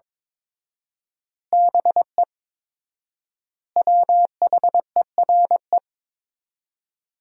Morse code, heard as a keyed tone, about 22 wpm, 700 Hz. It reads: BE WHERE
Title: 4x Word-Spacing - Sentences from Top 200 Words - 22wpm